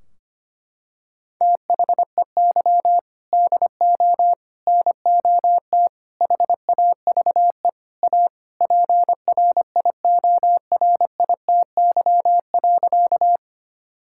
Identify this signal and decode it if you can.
Morse code, heard as a keyed tone, about 25 wpm, 700 Hz. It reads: THEY DO NOT HAVE A PRIORITY.
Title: Sentences from Top 1600 Words 25wpm